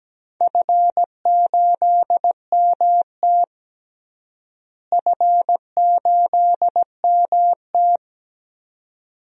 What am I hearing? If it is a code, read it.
Morse code, heard as a keyed tone, about 17 wpm, 700 Hz. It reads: F8MT F8MT
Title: DXCC-100 Call Signs - 4 Characters - Repeated 2x - 17wpm